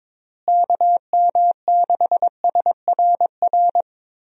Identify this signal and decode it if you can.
Morse code, heard as a keyed tone, about 22 wpm, 700 Hz. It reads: KM6SRR
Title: USA Call Signs - 6 Characters - Repeated 3x - 22wpm